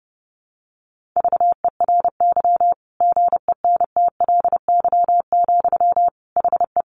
Morse code, heard as a keyed tone, about 30 wpm, 700 Hz. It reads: VERY GENTLY, HE
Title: Sentences from Top 1500 Words 30wpm